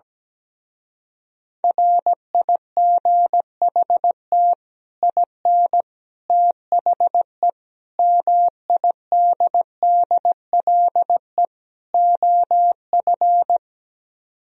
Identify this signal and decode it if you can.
Morse code, heard as a keyed tone, about 17 wpm, 700 Hz. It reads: RIGHT IN THE MIDDLE OF
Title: Speed-Racing - Sets of 5 Words 17wpm